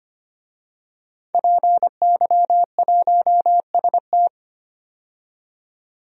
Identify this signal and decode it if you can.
Morse code, heard as a keyed tone, about 25 wpm, 700 Hz. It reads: PY1ST
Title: DXCC-100 Call Signs - 5 Characters - 25wpm